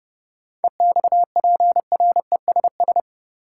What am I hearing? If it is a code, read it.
Morse code, heard as a keyed tone, about 30 wpm, 700 Hz. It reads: EXPRESS
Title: Sentences from Top 1100 Words - New Word Drill - 30wpm